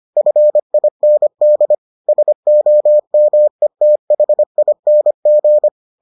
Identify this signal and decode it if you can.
Morse code, heard as a keyed tone, about 25 wpm, 600 Hz. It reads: FIND SOMETHING